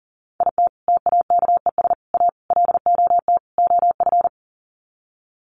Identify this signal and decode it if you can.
Morse code, heard as a keyed tone, about 40 wpm, 700 Hz. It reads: IT TAKES A LOT OF
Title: Sets of 5 Words Encore 40wpm